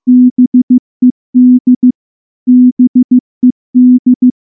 Morse code, heard as a keyed tone, about 14 wpm, 250 Hz. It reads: BED BED
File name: morse / morse-training-15WPM